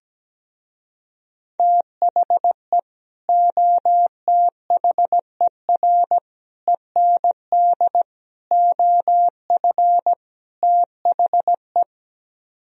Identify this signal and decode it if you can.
Morse code, heard as a keyed tone, about 17 wpm, 700 Hz. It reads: THE OTHER END OF THE
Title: Speed-Racing - Sets of 5 Words 17wpm